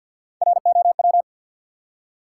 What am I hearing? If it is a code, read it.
Morse code, heard as a keyed tone, about 51 wpm, 700 Hz. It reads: ROW
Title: Three Letter Words - 50wpm